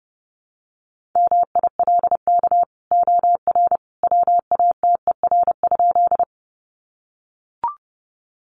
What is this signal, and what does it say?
Morse code, heard as a keyed tone, about 30 wpm, 700 Hz. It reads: MILK OR WATER?
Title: Sentences from Top 1500 Words 30wpm